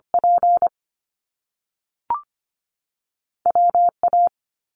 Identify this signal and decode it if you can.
Morse code, heard as a keyed tone, about 25 wpm, 700 Hz. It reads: P W A
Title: Top Two Letter Combinations in Words 25wpm